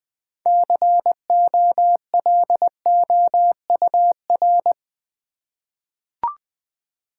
Morse code, heard as a keyed tone, about 20 wpm, 700 Hz. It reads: COLOUR
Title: 6 Letter Words - First 3 Letters Given - 20wpm